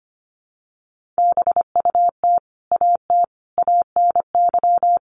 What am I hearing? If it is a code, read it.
Morse code, heard as a keyed tone, about 25 wpm, 700 Hz. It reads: BUT AT ANY
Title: Sentences from Top 700 Words 25wpm